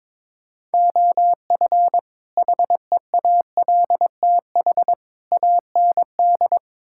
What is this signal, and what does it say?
Morse code, heard as a keyed tone, about 22 wpm, 700 Hz. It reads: OF HEALTH AND
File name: practice / 22wpm / Build A Phrase - 5-Word Phrases - 22wpm